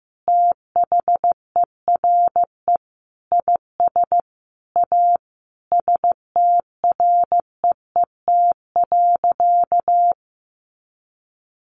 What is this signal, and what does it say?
Morse code, heard as a keyed tone, about 14 wpm, 700 Hz. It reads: THERE IS A STREET.
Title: Sentences from Top 500 Words 15wpm